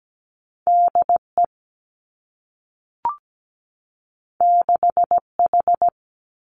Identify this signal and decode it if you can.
Morse code, heard as a keyed tone, about 17 wpm, 700 Hz. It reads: DE 6H
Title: Two Challenging Letter-Number Combinations 17wpm